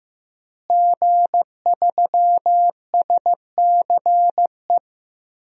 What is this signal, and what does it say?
Morse code, heard as a keyed tone, about 15 wpm, 700 Hz. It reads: G3SCE
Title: DXCC-100 Call Signs - 5 Characters - Repeated 2x - 15wpm